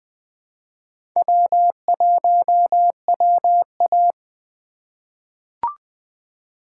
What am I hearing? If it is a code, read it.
Morse code, heard as a keyed tone, about 21 wpm, 700 Hz. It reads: W1WA
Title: USA Call Signs - 4 Characters - 3x Character Spacing - 20wpm